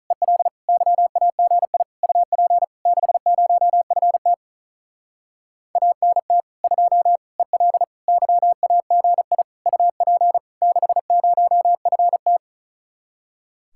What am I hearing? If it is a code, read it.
Morse code, heard as a keyed tone, about 44 wpm, 700 Hz. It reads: EL YAGI UP 60FT ANT 2 EL YAGI UP 60FT